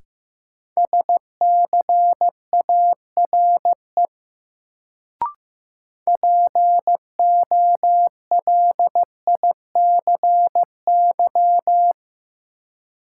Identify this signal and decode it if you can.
Morse code, heard as a keyed tone, about 15 wpm, 700 Hz. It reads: SCARE POLICY
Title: Sentences from Top 1000 Words - Review Part 5 - New Word Drill - 15wpm